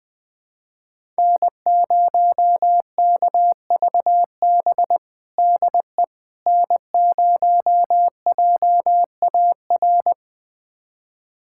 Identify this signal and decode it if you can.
Morse code, heard as a keyed tone, about 20 wpm, 700 Hz. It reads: N0KVB DE N0JAR
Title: USA Callsign DE Callsign - 5 Characters - 20wpm